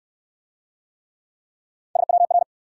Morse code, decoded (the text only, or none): SWR